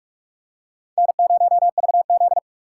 Morse code, heard as a keyed tone, about 40 wpm, 700 Hz. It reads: N0VZ